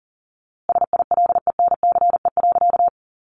Morse code, heard as a keyed tone, about 40 wpm, 700 Hz. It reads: SILENCE.
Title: A Sound Of Thunder 40wpm